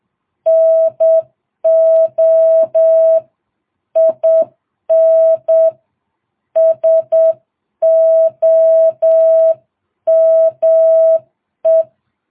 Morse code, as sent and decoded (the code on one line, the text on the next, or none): -. --- / .. -. / ... --- -- .
NO IN SOME